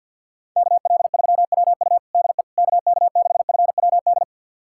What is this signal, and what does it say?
Morse code, heard as a keyed tone, about 52 wpm, 700 Hz. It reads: KL3WU DE KK6FWD
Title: USA Callsign DE Callsign 50wpm